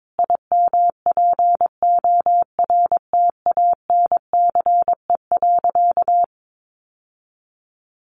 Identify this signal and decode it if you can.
Morse code, heard as a keyed tone, about 22 wpm, 700 Hz. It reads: IMPORTANCE.
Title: Sentences from Top 1200 Words 22wpm